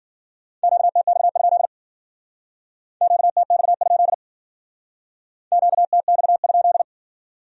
Morse code, heard as a keyed tone, about 50 wpm, 700 Hz. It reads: QTX? QTX? QTX?
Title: Speed-Racing Q-Codes - 40wpm